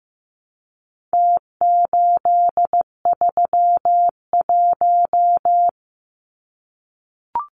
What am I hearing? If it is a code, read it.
Morse code, heard as a keyed tone, about 15 wpm, 700 Hz. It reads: T831
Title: Numbers - Contest Style - 4-Digit 15wpm